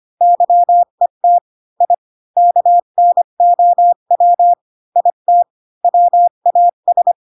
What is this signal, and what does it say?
Morse code, heard as a keyed tone, about 25 wpm, 700 Hz. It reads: YET I KNOW IT WAS